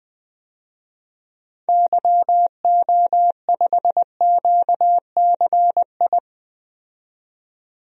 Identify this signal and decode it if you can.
Morse code, heard as a keyed tone, about 20 wpm, 700 Hz. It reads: YO5QCI
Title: DXCC-100 Call Signs - 6 Characters - 20wpm